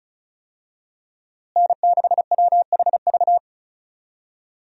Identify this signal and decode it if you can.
Morse code, heard as a keyed tone, about 34 wpm, 700 Hz. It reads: N6WHV
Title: USA Call Signs - 5 Characters - Repeated 2x - 35wpm